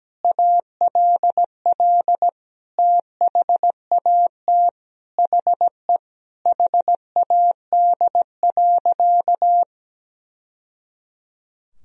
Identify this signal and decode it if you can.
Morse code, heard as a keyed tone, about 17 wpm, 700 Hz. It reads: ALL THAT HE HAD.
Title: Sentences from Top 1000 Words - Review Part 2 - 17wpm